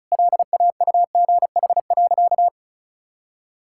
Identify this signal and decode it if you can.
Morse code, heard as a keyed tone, about 36 wpm, 700 Hz. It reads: LAUGH.
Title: Sentences from Top 1000 Words - Review Part 5 - 35wpm